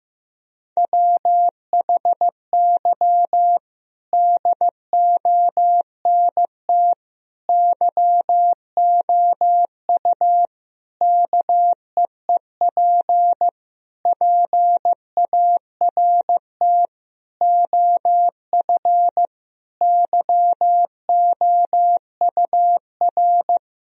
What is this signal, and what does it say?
Morse code, heard as a keyed tone, about 15 wpm, 700 Hz. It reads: WHY DONT YOU KEEP PART OF YOUR
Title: Sentences from Top 1000 Words - Review Part 4 - 15wpm